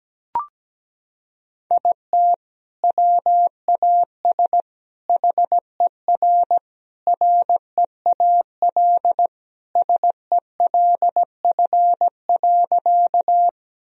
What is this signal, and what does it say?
Morse code, heard as a keyed tone, about 17 wpm, 700 Hz. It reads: IT WAS HER REAL SELF.
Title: Sentences from Top 1000 Words - Review Part 1 - 17wpm